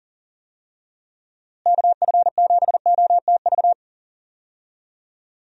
Morse code, heard as a keyed tone, about 40 wpm, 700 Hz. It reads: KF7OTV
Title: USA Call Signs - 6 Characters - Repeated 2x - 40wpm